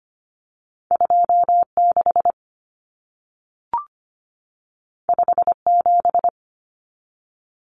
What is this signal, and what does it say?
Morse code, heard as a keyed tone, about 25 wpm, 700 Hz. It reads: 26 57
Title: Common Cut Numbers - 2-Digit 25wpm